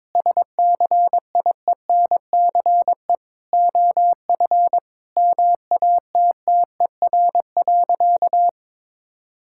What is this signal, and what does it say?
Morse code, heard as a keyed tone, about 22 wpm, 700 Hz. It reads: SCIENCE OF MATTER.
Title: Sentences from Top 1000 Words - Review Part 5 - 22wpm